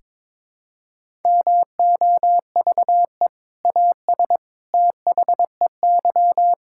Morse code, heard as a keyed tone, about 23 wpm, 700 Hz. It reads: MOVE AS THEY
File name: practice / 22wpm / Sentences with 2-Word Phrases - 22wpm